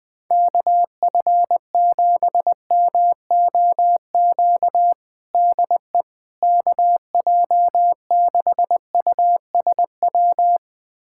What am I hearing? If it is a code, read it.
Morse code, heard as a keyed tone, about 20 wpm, 700 Hz. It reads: KF7MOQ DE KJ6USW
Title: USA Callsign DE Callsign - 6 Characters - 20wpm